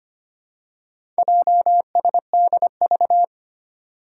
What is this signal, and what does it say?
Morse code, heard as a keyed tone, about 25 wpm, 700 Hz. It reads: JSDV
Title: Four Challenging Letter-Number Combinations 25wpm